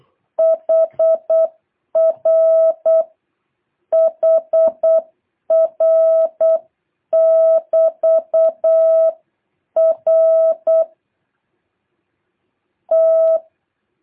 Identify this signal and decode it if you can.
Morse code, heard as a keyed tone, about 8 wpm, spaced as for 5 wpm, 650 Hz. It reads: HR HR <BT> RT
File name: jh8cbh / 2023cw